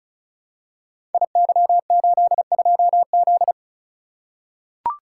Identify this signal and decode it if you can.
Morse code, heard as a keyed tone, about 36 wpm, 700 Hz. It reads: IY82Z
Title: Code Groups - 5 Letter:Numbers - 35wpm